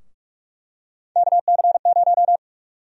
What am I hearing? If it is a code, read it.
Morse code, heard as a keyed tone, about 47 wpm, 700 Hz. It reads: KC0
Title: USA Call Sign Prefixes - Repeated 2x - 45wpm